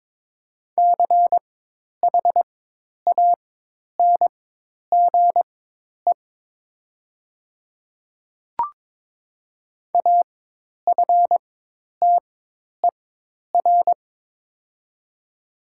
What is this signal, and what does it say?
Morse code, heard as a keyed tone, about 22 wpm, spaced as for 10 wpm, 700 Hz. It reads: CHANGE AFTER
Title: 4x Character Spacing - Top 300 Words - 22wpm